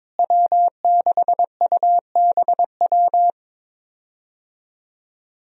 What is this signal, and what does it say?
Morse code, heard as a keyed tone, about 22 wpm, 700 Hz. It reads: W6UBW
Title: USA Call Signs - 5 Characters - Repeated 3x - 22wpm